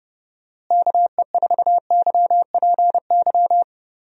Morse code, heard as a keyed tone, about 30 wpm, 700 Hz. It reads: KE4YPY
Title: USA Call Signs - 6 Characters - Repeated 3x - 30wpm